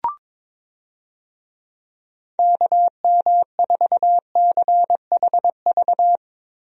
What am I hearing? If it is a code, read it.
Morse code, heard as a keyed tone, about 22 wpm, 700 Hz. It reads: KM4CHV
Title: USA Call Signs - 6 Characters - Repeated 2x - 22wpm